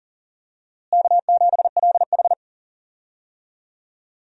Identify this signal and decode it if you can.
Morse code, heard as a keyed tone, about 40 wpm, 700 Hz. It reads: K7LH
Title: USA Call Signs - 4 Characters - Repeated 2x - 40wpm